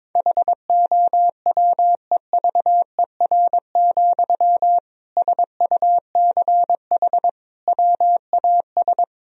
Morse code, heard as a keyed tone, about 21 wpm, 700 Hz. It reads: HOWEVER, SUCH WAS
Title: Sentences from Top 1000 Words - Review Part 3 - 22wpm